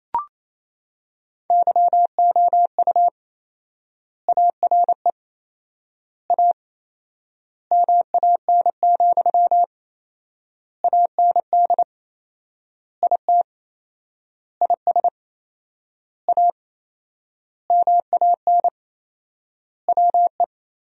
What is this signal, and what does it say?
Morse code, heard as a keyed tone, about 28 wpm, 700 Hz. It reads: YOU ARE A MAN, AND IT IS A MAN WE
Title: 4x Word-Spacing - Sentences from Top 200 Words - 28wpm